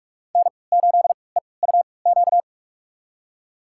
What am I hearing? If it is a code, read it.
Morse code, heard as a keyed tone, about 45 wpm, spaced as for 25 wpm, 700 Hz. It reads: N8EUQ
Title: USA Call Signs - 5 Characters - 3x Character Spacing - 45wpm